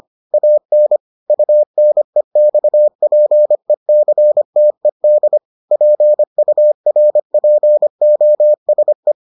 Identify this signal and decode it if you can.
Morse code, heard as a keyed tone, about 25 wpm, 600 Hz. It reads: AN UNEXPECTED PURPOSE